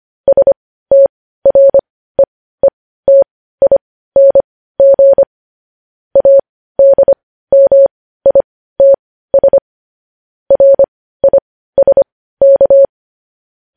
Morse code, heard as a keyed tone, about 25 wpm, spaced as for 15 wpm, 550 Hz. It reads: STREETING ADMITS RISK
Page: news-headlines-in-morse-code-at-15-wpm